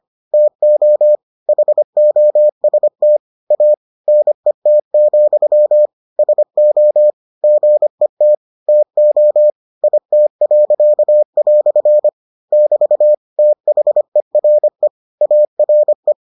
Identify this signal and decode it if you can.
Morse code, heard as a keyed tone, about 25 wpm, 600 Hz. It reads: TO HOST A NET, SO GET TO IT." <BT> THERE ARE